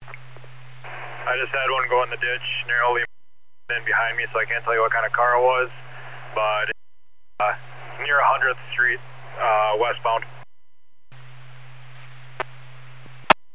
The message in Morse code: none

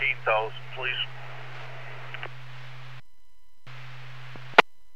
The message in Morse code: none

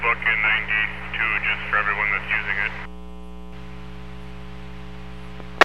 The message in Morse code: none

